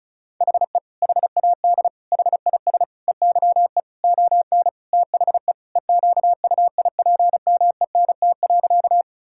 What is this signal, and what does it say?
Morse code, heard as a keyed tone, about 36 wpm, 700 Hz. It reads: HE HAD HIS EYE ON THE EQUIPMENT.
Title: Sentences from Top 1000 Words - Review Part 5 - 35wpm